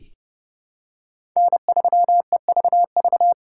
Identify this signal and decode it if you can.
Morse code, heard as a keyed tone, about 32 wpm, 700 Hz. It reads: N3EVV
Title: USA Call Signs - 5 Characters - 3x Character Spacing - 30wpm